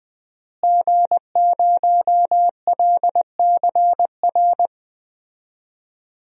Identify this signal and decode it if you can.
Morse code, heard as a keyed tone, about 21 wpm, 700 Hz. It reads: G0LCR
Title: DXCC-100 Call Signs - 5 Characters - Repeated 2x - 20wpm